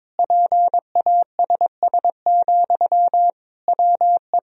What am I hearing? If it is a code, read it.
Morse code, heard as a keyed tone, about 22 wpm, 700 Hz. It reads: PASS, WE
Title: Sentences from Top 1000 Words - Review Part 2 - 22wpm